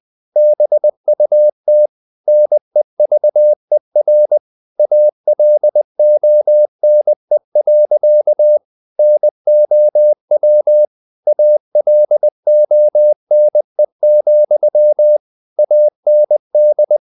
Morse code, heard as a keyed tone, about 20 wpm, 600 Hz. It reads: BUT NEVER ALONE. NOW ALONE, AND
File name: the old man and the sea / the old man and the sea part 1 20wpm